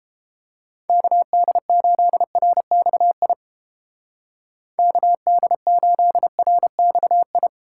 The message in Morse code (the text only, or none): KD8RXI KD8RXI